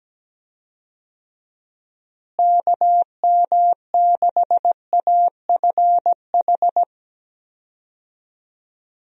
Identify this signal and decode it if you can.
Morse code, heard as a keyed tone, about 17 wpm, 700 Hz. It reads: KM6AFH
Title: USA Call Signs - 6 Characters - Repeated 2x - 17wpm